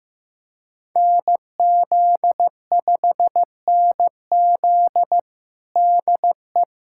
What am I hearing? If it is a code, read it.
Morse code, heard as a keyed tone, about 15 wpm, 700 Hz. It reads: NZ5NZ DE
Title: USA Callsign DE Callsign - 5 Characters - 15wpm